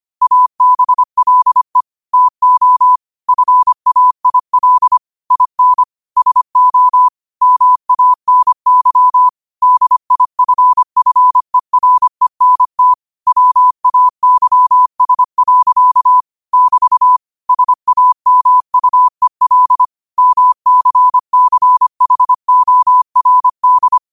Morse code, heard as a keyed tone, about 25 wpm, 1000 Hz. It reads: ABLE TO FAIL IN SO MANY DIFFERENT WAYS. <BT> SAMUEL MCCHORD